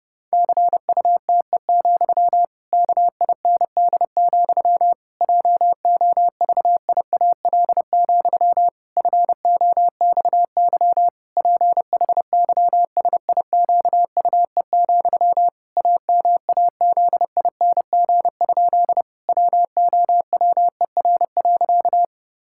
Morse code, heard as a keyed tone, about 30 wpm, 700 Hz. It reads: CUTE, KIND, JOVIAL, FOXY PHYSIQUE, AMAZING? WOWER.
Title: Pangram Challenge v2 - 30wpm